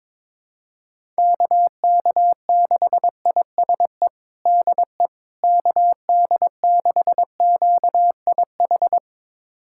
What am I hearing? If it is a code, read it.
Morse code, heard as a keyed tone, about 22 wpm, 700 Hz. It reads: KK6ISE DE KD6QIH
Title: USA Callsign DE Callsign - 6 Characters - 22wpm